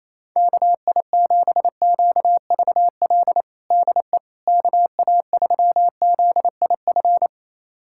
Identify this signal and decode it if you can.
Morse code, heard as a keyed tone, about 28 wpm, 700 Hz. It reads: KI7QVL DE KA3ZIF
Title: USA Callsign DE Callsign 28wpm